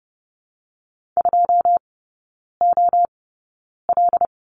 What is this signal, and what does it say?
Morse code, heard as a keyed tone, about 30 wpm, spaced as for 9 wpm, 700 Hz. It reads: 2OL
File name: Lesson 19e - Letter w - Three Characters 30wpm